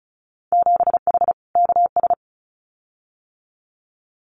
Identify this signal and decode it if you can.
Morse code, heard as a keyed tone, about 36 wpm, 700 Hz. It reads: 7H KS
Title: Winter Field Day Exchanges Encore 35wpm